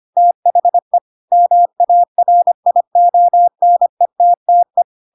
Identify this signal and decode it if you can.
Morse code, heard as a keyed tone, about 25 wpm, 700 Hz. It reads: THE MARIONETTE
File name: Pinocchio / Pinocchio ch3 25 25 ik1hge